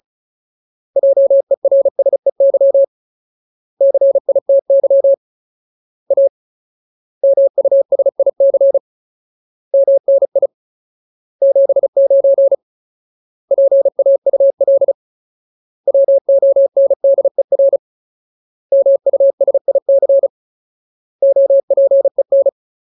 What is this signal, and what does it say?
Morse code, heard as a keyed tone, about 34 wpm, 550 Hz. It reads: JERSEY CITY A MUSIC MNI 79 PAUL WONDER MUSIC OPEN
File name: Ultimate Word List / 35WPM Ultimate